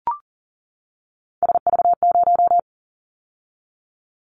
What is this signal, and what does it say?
Morse code, heard as a keyed tone, about 37 wpm, 700 Hz. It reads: SV0
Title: Callsign Prefixes for Top 50 DXCC - 40wpm